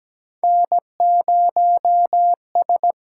Morse code, heard as a keyed tone, about 17 wpm, 700 Hz. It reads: N0S
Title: USA Call Signs - 8x Character Spacing - 17wpm